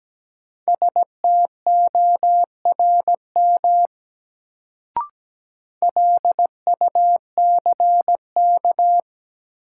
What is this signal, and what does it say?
Morse code, heard as a keyed tone, about 17 wpm, 700 Hz. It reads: STORM LUCK
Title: Sentences from Top 1500 Words - New Word Drill Alt - 17wpm